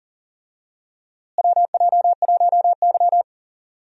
Morse code, heard as a keyed tone, about 40 wpm, 700 Hz. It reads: WJ1Y
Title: USA Call Signs - 4 Characters - Repeated 3x - 40wpm